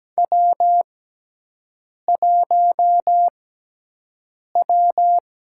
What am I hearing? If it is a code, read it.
Morse code, heard as a keyed tone, about 17 wpm, spaced as for 6 wpm, 700 Hz. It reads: W1W
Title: USA Call Signs - 4 Characters - 6x Character Spacing - 17wpm